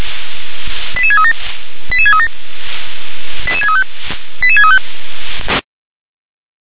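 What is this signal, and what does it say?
Five-tone selective call (ZVEI), heard as quick runs of five tones, tones of about 70 ms. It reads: 88637 88637 88635 88635
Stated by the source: ZVEI five-tone selective call received on an SDR.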